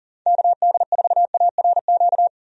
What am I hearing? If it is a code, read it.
Morse code, heard as a keyed tone, about 38 wpm, 700 Hz. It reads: KD4ARQ
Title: USA Call Signs - 6 Characters - 40wpm